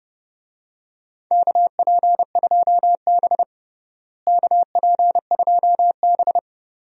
Morse code, heard as a keyed tone, about 31 wpm, 700 Hz. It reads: KP2B KP2B
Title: DXCC-100 Call Signs - 4 Characters - Repeated 3x - 30wpm